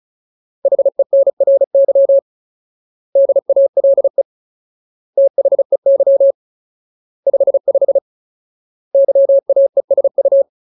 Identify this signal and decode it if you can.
Morse code, heard as a keyed tone, about 36 wpm, 550 Hz. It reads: HENRY DALE THEY 55 YAESU
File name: Ultimate Word List / 35WPM Ultimate